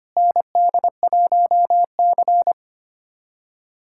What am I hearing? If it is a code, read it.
Morse code, heard as a keyed tone, about 25 wpm, 700 Hz. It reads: ND1C
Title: USA Callsign DE Callsign - 4 Characters - 25wpm